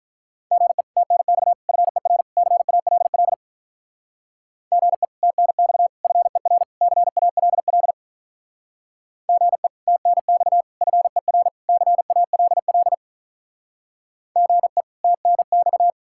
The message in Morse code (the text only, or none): GE TNX FER CALL GE TNX FER CALL GE TNX FER CALL GE TNX